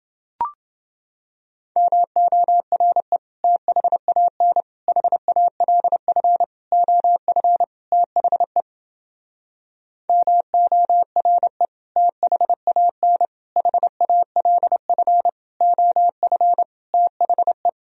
Morse code, heard as a keyed tone, about 29 wpm, 700 Hz. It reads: MORE THAN HALF OF THE MORE THAN HALF OF THE